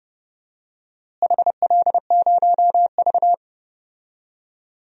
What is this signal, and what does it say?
Morse code, heard as a keyed tone, about 30 wpm, 700 Hz. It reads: HL0V
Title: DXCC-100 Call Signs - 4 Characters - Repeated 3x - 30wpm